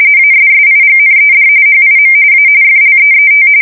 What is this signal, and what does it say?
RTTY, about 45.45 baud. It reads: ASSIGNED HIM A FLEET
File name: fsk170 / usnavy4